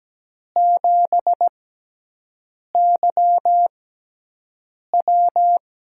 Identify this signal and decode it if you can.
Morse code, heard as a keyed tone, about 17 wpm, spaced as for 6 wpm, 700 Hz. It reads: 7YW
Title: USA Call Signs - 5 Characters - 6x Character Spacing - 17wpm